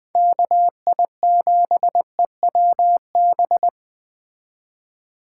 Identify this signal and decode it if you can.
Morse code, heard as a keyed tone, about 20 wpm, 700 Hz. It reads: KI7EWB